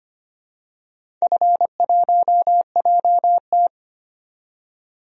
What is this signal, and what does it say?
Morse code, heard as a keyed tone, about 25 wpm, 690 Hz. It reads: F1JT
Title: DXCC-100 Call Signs - 4 Characters - 25wpm